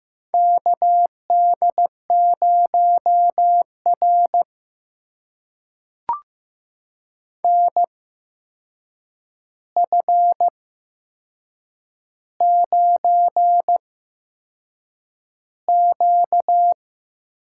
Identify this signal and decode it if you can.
Morse code, heard as a keyed tone, about 15 wpm, spaced as for 6 wpm, 700 Hz. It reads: KD0R NF9Q
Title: USA Call Signs - 4 Characters - 8x Character Spacing - 15wpm